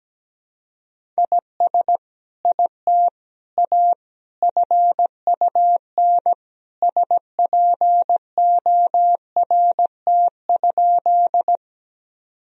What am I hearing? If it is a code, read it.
Morse code, heard as a keyed tone, about 17 wpm, 700 Hz. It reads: IS IT A FUN SPORT?
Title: Sentences from Top 1000 Words - Review Part 4 - 17wpm